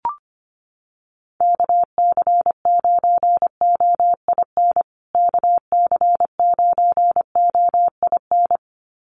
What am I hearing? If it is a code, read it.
Morse code, heard as a keyed tone, about 25 wpm, 700 Hz. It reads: KC9OIN KC9OIN
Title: USA Call Signs Repeated 2x 25wpm